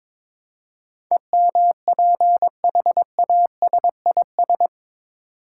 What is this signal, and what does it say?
Morse code, heard as a keyed tone, about 22 wpm, 700 Hz. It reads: EMPHASIS
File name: practice / 22wpm / Eight Letter Words - 22wpm